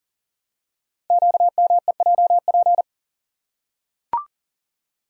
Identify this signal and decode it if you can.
Morse code, heard as a keyed tone, about 40 wpm, 700 Hz. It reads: QMEJP
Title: Code Groups - 5 Letter:Numbers - 40wpm